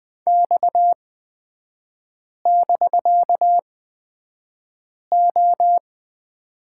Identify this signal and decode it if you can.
Morse code, heard as a keyed tone, about 20 wpm, spaced as for 5 wpm, 700 Hz. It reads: X<BK>O
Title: 250 Lesson 40e - BK - Three Characters 20wpm